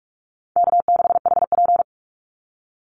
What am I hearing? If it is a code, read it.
Morse code, heard as a keyed tone, about 44 wpm, 700 Hz. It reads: K6HP
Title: Easy Call Signs Repeated 2x - 45wpm